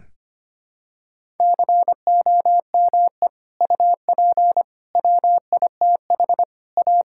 Morse code, heard as a keyed tone, about 25 wpm, 700 Hz. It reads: COME UP WITH A
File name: practice / 25wpm / Sentences with 3-Word Phrases - 25wpm